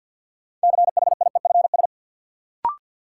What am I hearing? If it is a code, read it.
Morse code, heard as a keyed tone, about 50 wpm, 700 Hz. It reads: CHIEFS
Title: Sets of 5 Words Encore - New Word Drill - 50wpm